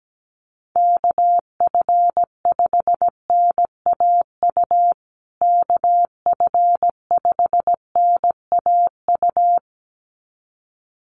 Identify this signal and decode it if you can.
Morse code, heard as a keyed tone, about 17 wpm, 700 Hz. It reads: KF5NAU KF5NAU
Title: USA Call Signs Repeated 2x 17wpm